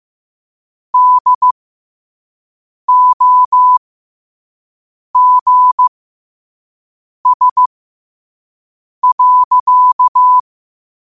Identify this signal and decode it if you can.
Morse code, heard as a keyed tone, about 15 wpm, spaced as for 5 wpm, 1000 Hz. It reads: DOGS.
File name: qotdfiles / 5wpm